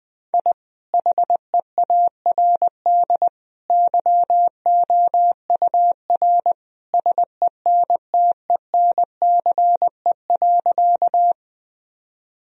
Morse code, heard as a keyed tone, about 20 wpm, 700 Hz. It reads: I HEARD YOUR SENTENCE.
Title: Sentences from Top 1000 Words - Review Part 1 - 20wpm